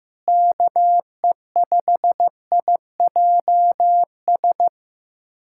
Morse code, heard as a keyed tone, about 16 wpm, 700 Hz. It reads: KE5IJS
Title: USA Callsign DE Callsign - 6 Characters - 15wpm